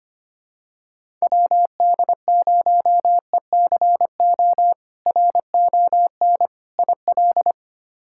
Morse code, heard as a keyed tone, about 25 wpm, 700 Hz. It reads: WD0ECO RON IL
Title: K1USN SST QSO Practice - 25wpm